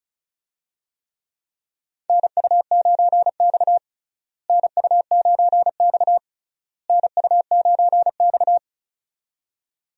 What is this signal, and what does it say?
Morse code, heard as a keyed tone, about 34 wpm, 700 Hz. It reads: NU9X NU9X NU9X
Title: USA Call Signs - 4 Characters - Repeated 3x - 35wpm